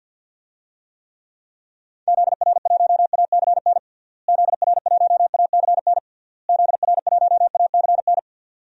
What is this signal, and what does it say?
Morse code, heard as a keyed tone, about 51 wpm, 700 Hz. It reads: ZR1ACN ZR1ACN ZR1ACN
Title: DXCC-100 Call Signs - 6 Characters - Repeated 3x - 50wpm